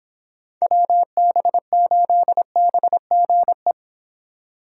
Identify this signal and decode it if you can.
Morse code, heard as a keyed tone, about 25 wpm, 700 Hz. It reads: WB8BGE